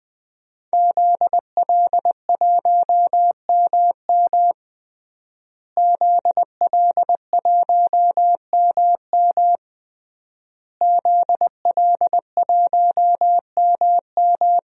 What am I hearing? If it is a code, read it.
Morse code, heard as a keyed tone, about 21 wpm, 700 Hz. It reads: ZL1MM ZL1MM ZL1MM
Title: DXCC-100 Call Signs - 5 Characters - Repeated 3x - 20wpm